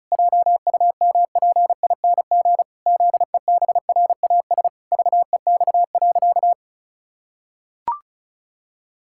Morse code, heard as a keyed tone, about 34 wpm, 700 Hz. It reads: JUMPING ZEBRAS VEX.
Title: Pangram Challenge v2 - 35wpm